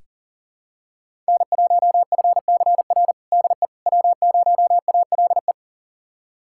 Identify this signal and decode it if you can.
Morse code, heard as a keyed tone, about 37 wpm, 700 Hz. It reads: N1FCR DE W0ALE
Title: USA Callsign DE Callsign - 5 Characters - 40wpm